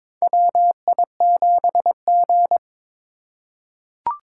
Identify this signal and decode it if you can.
Morse code, heard as a keyed tone, about 22 wpm, 700 Hz. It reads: WI7G
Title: USA Call Signs - 4 Characters - 3x Character Spacing - 22wpm